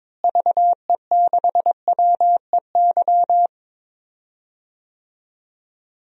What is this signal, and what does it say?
Morse code, heard as a keyed tone, about 22 wpm, 700 Hz. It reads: VE6WEY
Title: DXCC-100 Call Signs - 6 Characters - Repeated 2x - 22wpm